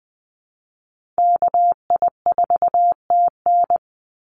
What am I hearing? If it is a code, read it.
Morse code, heard as a keyed tone, about 20 wpm, 700 Hz. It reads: KI4TN
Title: NA SOTA Activator Callsigns 20wpm